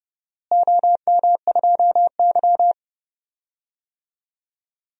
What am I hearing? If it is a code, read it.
Morse code, heard as a keyed tone, about 30 wpm, 700 Hz. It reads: OM2Y
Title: DXCC-100 Call Signs - 4 Characters - Repeated 2x - 30wpm